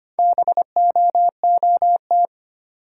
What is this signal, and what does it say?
Morse code, heard as a keyed tone, about 25 wpm, 700 Hz. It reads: BOOT
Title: Four Letter Words - 25wpm